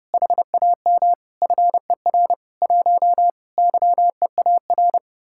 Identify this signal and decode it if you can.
Morse code, heard as a keyed tone, about 30 wpm, 700 Hz. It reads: HAM FER 1 YEAR